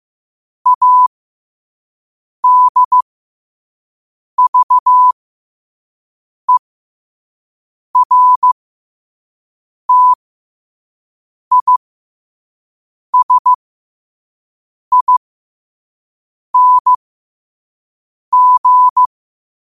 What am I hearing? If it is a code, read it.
Morse code, heard as a keyed tone, about 15 wpm, spaced as for 5 wpm, 1000 Hz. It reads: ADVERTISING
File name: qotdfiles / 5wpm